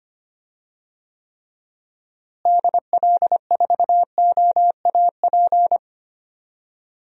Morse code, heard as a keyed tone, about 25 wpm, 700 Hz. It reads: DL4OAP
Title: USA Call Signs - 6 Characters - Repeated 3x - 25wpm